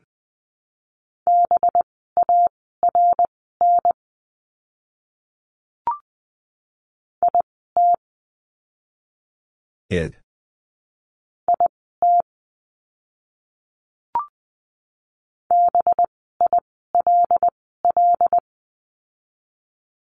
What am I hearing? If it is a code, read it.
Morse code, heard as a keyed tone, about 20 wpm, spaced as for 15 wpm, 700 Hz. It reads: BARN IT IT BILL
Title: 185 Lesson 30f - Letter b - Words 20wpm